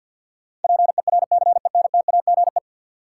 Morse code, heard as a keyed tone, about 52 wpm, 700 Hz. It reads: PERCENTAGE
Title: Sentences from Top 1300 Words - New Word Drill - 50wpm